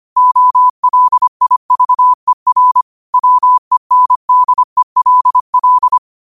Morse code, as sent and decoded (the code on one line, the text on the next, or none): --- .-.. .. ...- . .-. / .-- . -. -.. .-.. .-..
OLIVER WENDELL